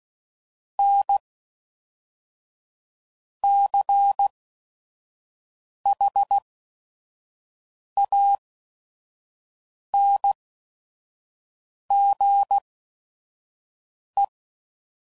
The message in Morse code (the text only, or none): N C H A N G E